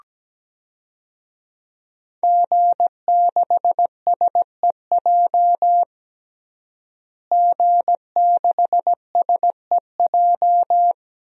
Morse code, heard as a keyed tone, about 17 wpm, 700 Hz. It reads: G6SEJ G6SEJ
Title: DXCC-100 Call Signs - 5 Characters - Repeated 2x - 17wpm